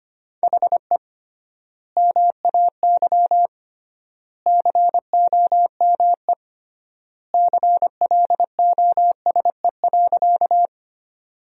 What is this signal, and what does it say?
Morse code, heard as a keyed tone, about 25 wpm, 700 Hz. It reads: HE MAY COME CLOSE.
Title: 3x Word-Spacing - Sentences from Top 300 Words - 25wpm